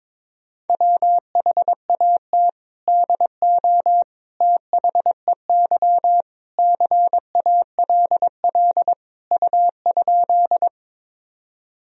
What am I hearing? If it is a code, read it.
Morse code, heard as a keyed tone, about 22 wpm, 700 Hz. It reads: WHAT DO THEY CALL U?
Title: Sentences from Top 100 Words and Abbreviations 22wpm